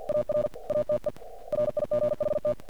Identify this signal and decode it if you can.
Morse code, heard as a keyed tone, about 48 wpm, 590 Hz. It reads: AR ATE RIGHT
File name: archived / CRSnet